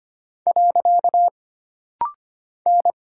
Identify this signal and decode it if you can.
Morse code, heard as a keyed tone, about 25 wpm, 700 Hz. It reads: . N
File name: Single Character (Letter:Number:Punctuation:Prosign) - Rapid-Fire - 25wpm